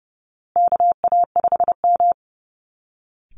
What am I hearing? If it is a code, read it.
Morse code, heard as a keyed tone, about 30 wpm, 700 Hz. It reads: KA5M
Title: Call Signs Repeated 2x - 30wpm